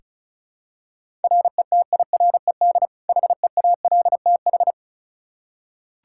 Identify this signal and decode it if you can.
Morse code, heard as a keyed tone, about 36 wpm, 700 Hz. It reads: RETIRED HEALTH
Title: Practice QSO - Exchange 3 - More Info - 35wpm